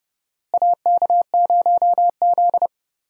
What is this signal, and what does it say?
Morse code, heard as a keyed tone, about 30 wpm, 700 Hz. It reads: AK0Z
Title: USA Call Signs - 4 Characters - Repeated 2x - 30wpm